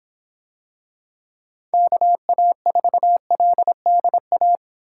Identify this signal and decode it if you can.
Morse code, heard as a keyed tone, about 26 wpm, 700 Hz. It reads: KA4LDA